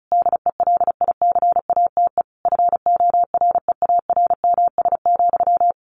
Morse code, heard as a keyed tone, about 34 wpm, 700 Hz. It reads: DELICATE FOREARMS,
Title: A Sound Of Thunder 35wpm